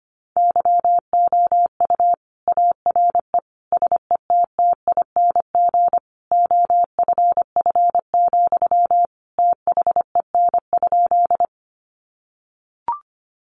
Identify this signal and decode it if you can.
Morse code, heard as a keyed tone, about 25 wpm, 700 Hz. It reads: YOU ARE SETTING OFF, THEN?
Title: Sentences from Top 1300 Words 25wpm